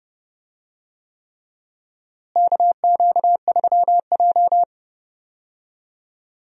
Morse code, heard as a keyed tone, about 30 wpm, 700 Hz. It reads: KQ3J
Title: USA Call Signs - 4 Characters - Repeated 3x - 30wpm